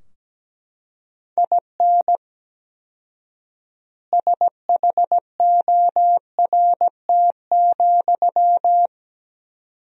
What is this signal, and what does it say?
Morse code, heard as a keyed tone, about 17 wpm, 700 Hz. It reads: IN SHORT,
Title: 4x Word-Spacing - Sentences from Top 400 Words - 17wpm